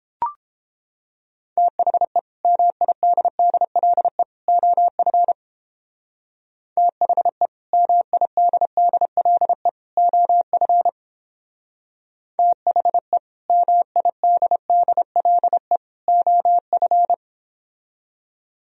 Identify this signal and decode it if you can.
Morse code, heard as a keyed tone, about 29 wpm, 700 Hz. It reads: THE MIDDLE OF THE MIDDLE OF THE MIDDLE OF